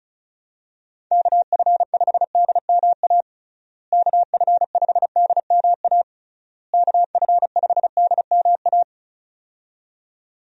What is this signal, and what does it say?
Morse code, heard as a keyed tone, about 34 wpm, 700 Hz. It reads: KF5DMA KF5DMA KF5DMA
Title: USA Call Signs - 6 Characters - Repeated 3x - 35wpm